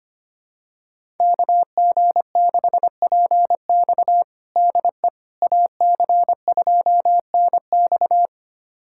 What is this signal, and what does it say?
Morse code, heard as a keyed tone, about 25 wpm, 700 Hz. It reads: KG6PX DE AC2NX
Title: USA Callsign DE Callsign - 5 Characters - 25wpm